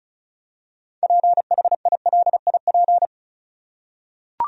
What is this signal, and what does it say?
Morse code, heard as a keyed tone, about 34 wpm, 700 Hz. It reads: PHILIP
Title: USA Common Names - 35wpm